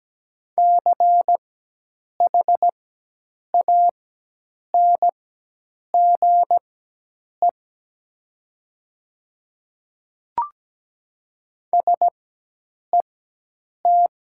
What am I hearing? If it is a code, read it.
Morse code, heard as a keyed tone, about 17 wpm, spaced as for 8 wpm, 700 Hz. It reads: CHANGE SET
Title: 4x Character Spacing - Top 200 Words - 17wpm